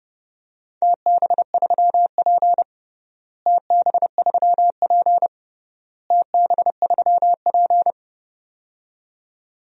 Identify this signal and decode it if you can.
Morse code, heard as a keyed tone, about 30 wpm, 700 Hz. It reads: TB3P TB3P TB3P
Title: DXCC-100 Call Signs - 4 Characters - Repeated 3x - 30wpm